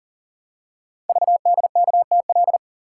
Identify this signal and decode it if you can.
Morse code, heard as a keyed tone, about 42 wpm, 700 Hz. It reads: VDKTL
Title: Code Groups - 5 Letters - 40wpm